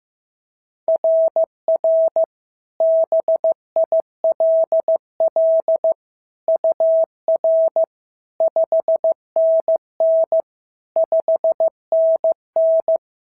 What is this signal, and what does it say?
Morse code, heard as a keyed tone, about 14 wpm, 650 Hz. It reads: RR BILL UR 5NN 5NN
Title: SOTA QSO Practice 15wpm